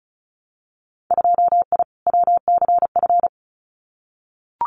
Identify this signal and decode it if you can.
Morse code, heard as a keyed tone, about 36 wpm, 700 Hz. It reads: 2I WCF
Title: Winter Field Day Exchanges 35wpm